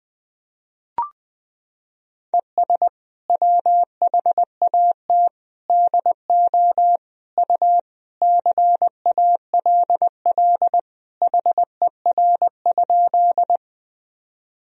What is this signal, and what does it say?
Morse code, heard as a keyed tone, about 20 wpm, 700 Hz. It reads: ES WHAT DO U CALL HER?
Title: Sentences from Top 100 Words and Abbreviations 20wpm